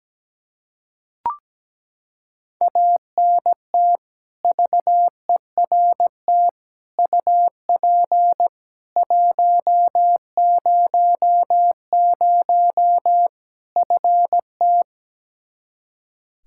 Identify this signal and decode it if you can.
Morse code, heard as a keyed tone, about 17 wpm, 700 Hz. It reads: ANT VERT UP 100 FT
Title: Practice QSO - Exchange 2 - Info - 17wpm